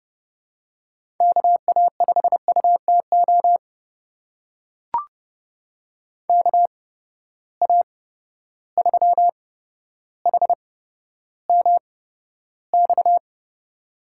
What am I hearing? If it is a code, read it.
Morse code, heard as a keyed tone, about 29 wpm, spaced as for 13 wpm, 700 Hz. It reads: KA5UTO KA3HMX
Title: USA Call Signs - 8x Character Spacing - 30wpm